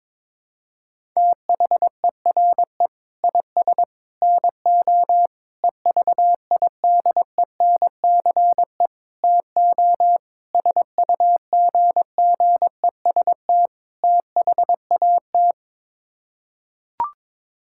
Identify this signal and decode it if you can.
Morse code, heard as a keyed tone, about 21 wpm, 700 Hz. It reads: THERE IS NO EVIDENCE TO SUGGEST THAT
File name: practice / 22wpm / Sets of 7 Words - 22wpm